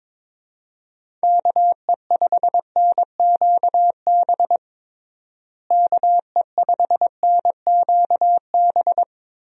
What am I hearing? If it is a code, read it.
Morse code, heard as a keyed tone, about 22 wpm, 700 Hz. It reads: KE5NQB KE5NQB
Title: USA Call Signs - 6 Characters - Repeated 2x - 22wpm